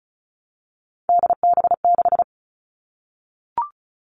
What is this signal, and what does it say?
Morse code, heard as a keyed tone, about 34 wpm, 700 Hz. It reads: DB6
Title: All Cut Numbers - 3-Digit 35wpm